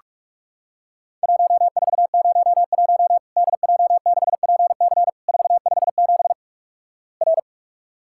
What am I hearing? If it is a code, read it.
Morse code, heard as a keyed tone, about 46 wpm, 700 Hz. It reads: 1401 DJ6PC 457 R
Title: Worked All Europe DX Contest - QTC Exchange - 45wpm